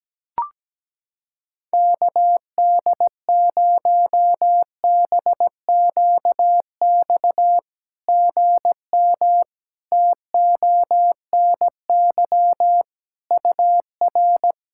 Morse code, heard as a keyed tone, about 17 wpm, 700 Hz. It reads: KD0BQX GM TONY UR
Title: SOTA QSO Practice 17wpm